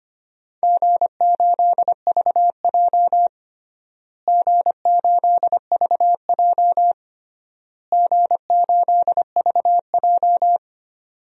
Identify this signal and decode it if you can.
Morse code, heard as a keyed tone, about 25 wpm, 700 Hz. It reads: G8VJ G8VJ G8VJ
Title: DXCC-100 Call Signs - 4 Characters - Repeated 3x - 25wpm